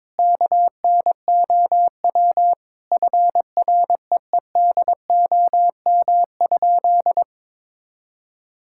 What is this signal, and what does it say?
Morse code, heard as a keyed tone, about 22 wpm, 700 Hz. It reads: KNOW FREEDOM?
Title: Sentences from Top 1000 Words - Review Part 5 - 22wpm